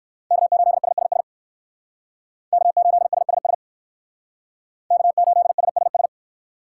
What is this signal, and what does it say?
Morse code, heard as a keyed tone, about 54 wpm, 700 Hz. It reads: K8SSS K8SSS K8SSS
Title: Speed Racing - USA 1x3 Call Signs 45wpm